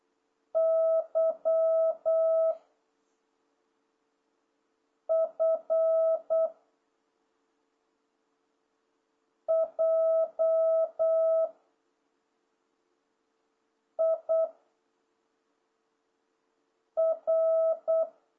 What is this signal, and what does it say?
Morse code, heard as a keyed tone, about 8 wpm, 660 Hz. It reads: Y F J I R